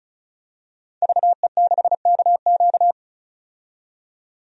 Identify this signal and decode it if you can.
Morse code, heard as a keyed tone, about 36 wpm, 700 Hz. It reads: VE6KQ
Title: DXCC-100 Call Signs - 5 Characters - 35wpm